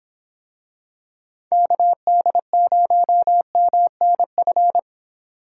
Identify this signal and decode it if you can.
Morse code, heard as a keyed tone, about 26 wpm, 700 Hz. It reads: KD0MNF